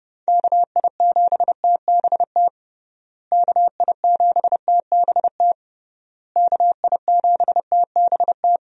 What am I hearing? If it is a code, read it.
Morse code, heard as a keyed tone, about 30 wpm, 700 Hz. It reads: KI7TBT KI7TBT KI7TBT
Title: USA Call Signs - 6 Characters - Repeated 3x - 30wpm